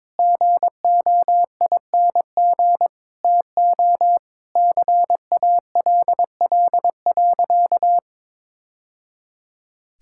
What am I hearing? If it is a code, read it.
Morse code, heard as a keyed tone, about 22 wpm, 700 Hz. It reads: GOING TO CALL.
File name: practice / 22wpm / Sentences with 3-Word Phrases - 22wpm